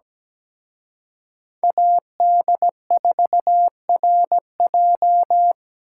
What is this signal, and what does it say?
Morse code, heard as a keyed tone, about 20 wpm, 700 Hz. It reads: AD4RJ
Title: Speed Racing - USA 2x2 Call Signs 17wpm